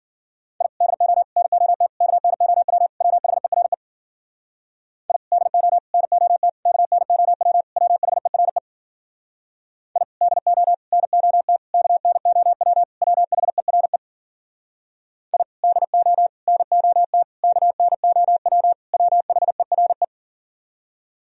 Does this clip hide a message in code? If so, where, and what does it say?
Yes: Morse code from 0.6 to 20.0 s, heard as a keyed tone, about 52 wpm, 700 Hz. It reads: I DO NOT KNOW WHERE I DO NOT KNOW WHERE I DO NOT KNOW WHERE I DO NOT KNOW WHERE